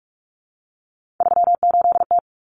Morse code, heard as a keyed tone, about 44 wpm, 700 Hz. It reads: 38T
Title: Common Cut Numbers - 3-Digit 45wpm